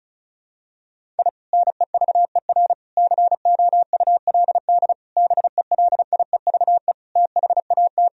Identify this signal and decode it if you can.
Morse code, heard as a keyed tone, about 36 wpm, 700 Hz. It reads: I NEVER COULD BELIEVE THAT
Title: Sentences from Top 1000 Words - Review Part 1 - 35wpm